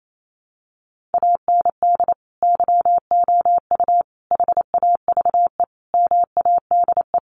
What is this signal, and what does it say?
Morse code, heard as a keyed tone, about 28 wpm, 700 Hz. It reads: AND YOU HAVE MADE
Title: Sentences from Top 1500 Words 28wpm